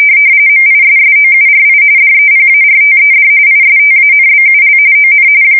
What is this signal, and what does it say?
RTTY, about 45.45 baud. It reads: MISSOURI DEPARTED ISTANBUL ON
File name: fsk170 / USS-Missouri